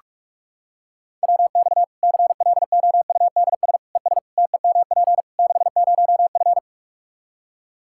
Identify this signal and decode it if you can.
Morse code, heard as a keyed tone, about 46 wpm, 700 Hz. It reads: WX CLOUDS ES TEMP 60F
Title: Practice QSO - Exchange 2 - Info - 45wpm